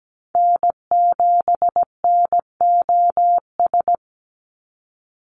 Morse code, heard as a keyed tone, about 17 wpm, 700 Hz. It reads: N7NOS
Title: USA Call Signs 17wpm